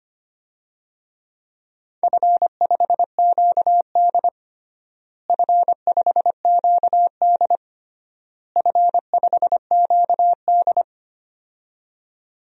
Morse code, heard as a keyed tone, about 25 wpm, 700 Hz. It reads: F5QD F5QD F5QD
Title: DXCC-100 Call Signs - 4 Characters - Repeated 3x - 25wpm